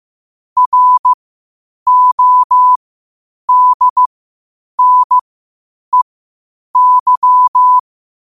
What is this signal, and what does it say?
Morse code, heard as a keyed tone, about 15 wpm, spaced as for 8 wpm, 1000 Hz. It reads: RODNEY
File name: qotdfiles / 8wpm